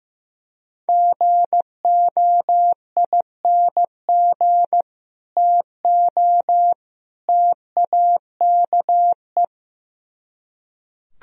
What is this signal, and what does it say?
Morse code, heard as a keyed tone, about 17 wpm, 700 Hz. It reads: GOING TO TAKE
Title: Speed-Racing - Sets of 3 Words 15wpm